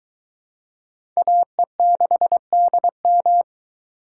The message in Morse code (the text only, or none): AE6DM